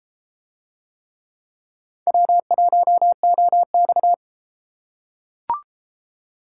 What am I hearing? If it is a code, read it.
Morse code, heard as a keyed tone, about 32 wpm, 700 Hz. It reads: W1OX